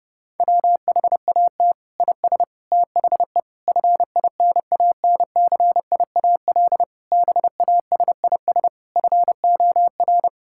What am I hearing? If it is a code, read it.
Morse code, heard as a keyed tone, about 32 wpm, 700 Hz. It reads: WHAT IS THE FINANCIAL BASIS FOR
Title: Sentences from Top 1000 Words - Review Part 5 - 30wpm